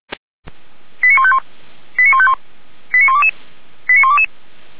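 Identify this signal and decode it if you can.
Five-tone selective call (ZVEI), heard as quick runs of five tones, tones of about 70 ms. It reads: 89151 89151 89122 89122
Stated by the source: ZVEI five-tone selective call received on SDR.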